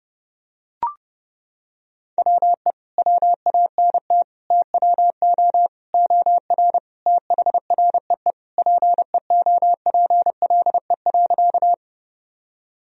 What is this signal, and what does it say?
Morse code, heard as a keyed tone, about 30 wpm, 700 Hz. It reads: WE WANT TWO OR THREE PEOPLE.
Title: Sentences from Top 1000 Words - Review Part 1 - 30wpm